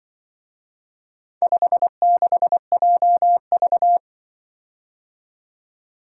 Morse code, heard as a keyed tone, about 24 wpm, 700 Hz. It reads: 56JV
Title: Speed-Racing - Four Letter-Number Combinations - 20wpm